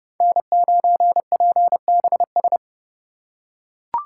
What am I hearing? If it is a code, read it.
Morse code, heard as a keyed tone, about 29 wpm, 700 Hz. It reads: N9PBS
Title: USA Call Signs - 3x Character Spacing - 30wpm